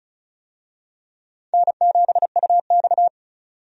Morse code, heard as a keyed tone, about 36 wpm, 700 Hz. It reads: N7UX
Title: USA Call Signs - 4 Characters - 35wpm